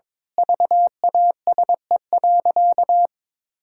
Morse code, heard as a keyed tone, about 22 wpm, 700 Hz. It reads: VASE.
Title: Pangram Challenge v2 - 22wpm